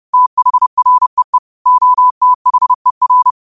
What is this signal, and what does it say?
Morse code, heard as a keyed tone, about 30 wpm, 1000 Hz. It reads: THREE OTHER